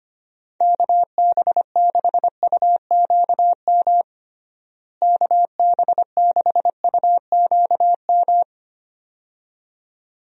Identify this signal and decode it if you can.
Morse code, heard as a keyed tone, about 25 wpm, 700 Hz. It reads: KB6UQM KB6UQM
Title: USA Call Signs - 6 Characters - Repeated 2x - 25wpm